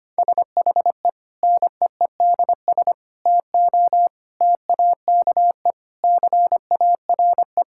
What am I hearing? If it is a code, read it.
Morse code, heard as a keyed tone, about 25 wpm, 700 Hz. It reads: SHE NEEDS TO TAKE CARE
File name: practice / 25wpm / Sentences with 4-Word Phrases - 25wpm